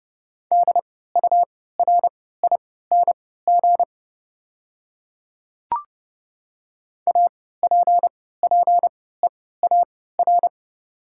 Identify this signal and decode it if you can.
Morse code, heard as a keyed tone, about 30 wpm, spaced as for 17 wpm, 700 Hz. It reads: DURING APPEAR
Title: 3x Character Spacing - Top 500 Words - 30wpm